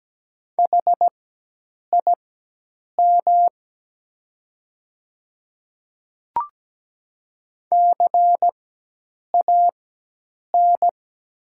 Morse code, heard as a keyed tone, about 17 wpm, spaced as for 8 wpm, 700 Hz. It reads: HIM CAN
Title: 4x Character Spacing - Top 100 Words - 17wpm